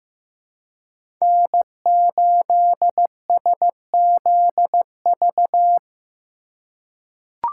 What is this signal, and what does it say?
Morse code, heard as a keyed tone, about 15 wpm, 700 Hz. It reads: N8SZV
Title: USA Call Signs - 5 Characters - 4x Character Spacing - 15wpm